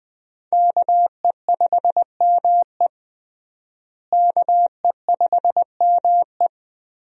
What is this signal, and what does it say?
Morse code, heard as a keyed tone, about 20 wpm, 700 Hz. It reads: KE5ME KE5ME
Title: USA Call Signs - 5 Characters - Repeated 3x - 20wpm